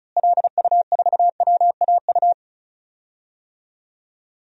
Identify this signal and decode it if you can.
Morse code, heard as a keyed tone, about 36 wpm, 700 Hz. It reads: LU4WAU
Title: DXCC-100 Call Signs - 6 Characters - Repeated 3x - 35wpm